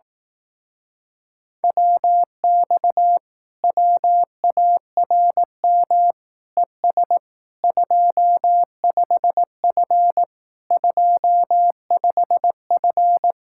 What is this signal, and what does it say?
Morse code, heard as a keyed tone, about 18 wpm, 700 Hz. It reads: WX WARM ES 25F 25F
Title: Speed-Racing Common QSO Phrases Encore - 15wpm